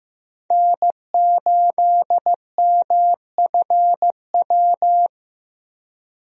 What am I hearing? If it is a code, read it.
Morse code, heard as a keyed tone, about 15 wpm, 700 Hz. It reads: N8MFW
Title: USA Call Signs - 5 Characters - Repeated 3x - 15wpm